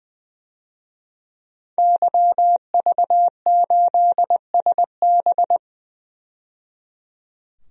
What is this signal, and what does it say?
Morse code, heard as a keyed tone, about 21 wpm, 700 Hz. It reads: YV8SB
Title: DXCC-100 Call Signs - 5 Characters - 20wpm